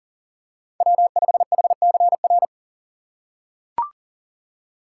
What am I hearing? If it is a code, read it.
Morse code, heard as a keyed tone, about 40 wpm, 700 Hz. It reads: W5HCR
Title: USA Call Signs - 5 Characters - 3x Character Spacing - 40wpm